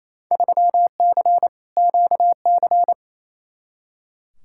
Morse code, heard as a keyed tone, about 28 wpm, 700 Hz. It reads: 3C QC
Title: Field Day Exchanges Encore - 28wpm